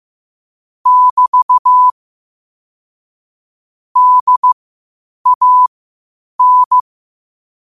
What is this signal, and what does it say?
Morse code, heard as a keyed tone, about 15 wpm, spaced as for 8 wpm, 1000 Hz. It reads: <BT> DAN
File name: qotdfiles / 8wpm